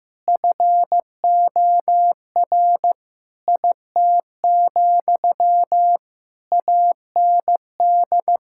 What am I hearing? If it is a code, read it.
Morse code, heard as a keyed tone, about 15 wpm, 700 Hz. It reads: FOR IT, AND
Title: Sentences from Top 1000 Words - Review Part 5 - 15wpm